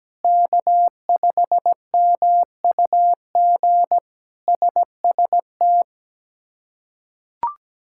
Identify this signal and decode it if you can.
Morse code, heard as a keyed tone, about 17 wpm, 700 Hz. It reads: K5MUG SST
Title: K1USN SST QSO Practice - 17wpm